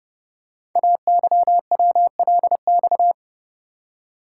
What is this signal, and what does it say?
Morse code, heard as a keyed tone, about 28 wpm, 700 Hz. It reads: AYWLX
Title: Code Groups - 5 Letters - 30wpm